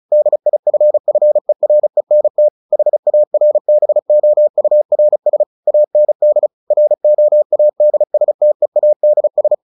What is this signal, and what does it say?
Morse code, heard as a keyed tone, about 34 wpm, 600 Hz. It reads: DIFFERENT HARBOURS AND ROADSTEADS
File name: the old man and the sea / the old man and the sea 1 35wpm